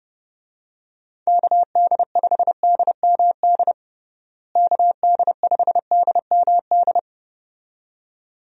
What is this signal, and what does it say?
Morse code, heard as a keyed tone, about 30 wpm, 700 Hz. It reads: KD5DMD KD5DMD
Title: USA Call Signs - 6 Characters - Repeated 2x - 30wpm